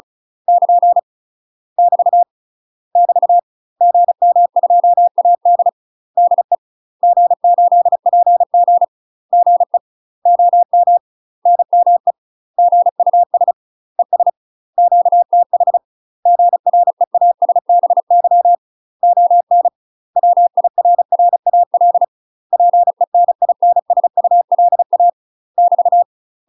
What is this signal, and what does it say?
Morse code, heard as a keyed tone, about 34 wpm, 700 Hz. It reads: <KN> <BT> <BT> GM2AD DE G8PG GE OM NME GUS ES QTH GREASBY ON WIRRAL PENINSULA <BT>
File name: UK MASTER QSOs TRAINING / 35WPM / UK MASTER QSOs 0000